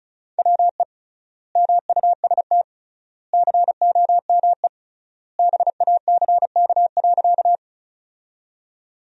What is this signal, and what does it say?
Morse code, heard as a keyed tone, about 36 wpm, 700 Hz. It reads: WE MUST COME BACK.
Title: 3x Word-Spacing - Sentences from Top 200 Words - 35wpm